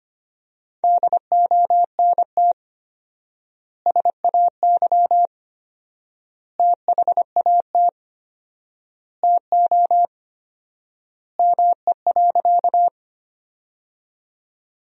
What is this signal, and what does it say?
Morse code, heard as a keyed tone, about 25 wpm, 700 Hz. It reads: DONT SAY THAT TO ME.
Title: 4x Word-Spacing - Sentences from Top 300 Words - 25wpm